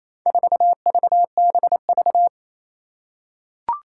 Morse code, heard as a keyed tone, about 28 wpm, 700 Hz. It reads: 4VBV
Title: Four Challenging Letter-Number Combinations 28wpm